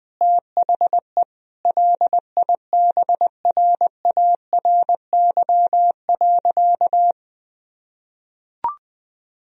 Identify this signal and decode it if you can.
Morse code, heard as a keyed tone, about 20 wpm, 700 Hz. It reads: THE LIBRARY.
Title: Sentences from Top 1000 Words - Review Part 4 - 20wpm